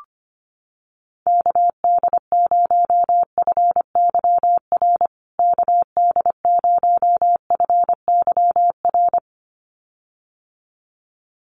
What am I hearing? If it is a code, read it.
Morse code, heard as a keyed tone, about 25 wpm, 700 Hz. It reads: KD0FYR KD0FYR
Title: USA Call Signs Repeated 2x 25wpm